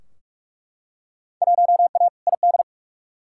Morse code, heard as a keyed tone, about 42 wpm, 700 Hz. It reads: 1A ID
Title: Field Day Exchanges Encore - 45wpm